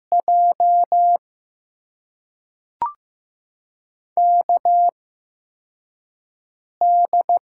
Morse code, heard as a keyed tone, about 15 wpm, spaced as for 4 wpm, 700 Hz. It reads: J KD
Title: USA Call Signs - 6 Characters - 8x Character Spacing - 15wpm